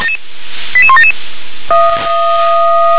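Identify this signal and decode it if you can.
Five-tone selective call (ZVEI), heard as quick runs of five tones, tones of about 70 ms. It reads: 88177
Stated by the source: Five-tone ZVEI selective call received by SDR.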